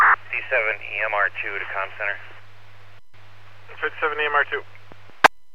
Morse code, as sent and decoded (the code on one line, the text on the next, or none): none